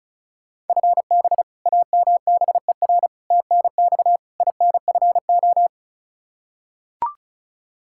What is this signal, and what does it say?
Morse code, heard as a keyed tone, about 36 wpm, 700 Hz. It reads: FB AMBER TNX INFO
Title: Practice QSO - Exchange 3 - More Info - 35wpm